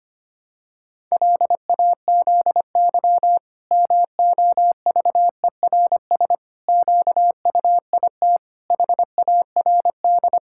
Morse code, heard as a keyed tone, about 25 wpm, 700 Hz. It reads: LAZY MOVERS QUIT HARD
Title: Pangram Challenge v2 - 25wpm